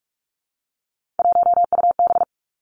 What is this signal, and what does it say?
Morse code, heard as a keyed tone, about 46 wpm, 700 Hz. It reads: 1UB
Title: Three Challenging Letter-Number Combinations 45wpm